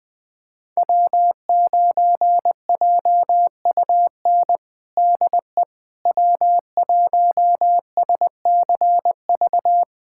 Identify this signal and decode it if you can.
Morse code, heard as a keyed tone, about 21 wpm, 700 Hz. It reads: W9JUN DE W1SCV
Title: USA Callsign DE Callsign - 5 Characters - 20wpm